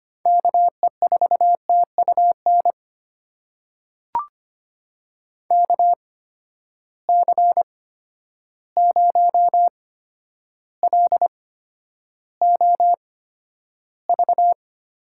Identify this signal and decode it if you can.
Morse code, heard as a keyed tone, about 25 wpm, spaced as for 11 wpm, 700 Hz. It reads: KE4TUN KC0LOV
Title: USA Call Signs - 6 Characters - 8x Character Spacing - 25wpm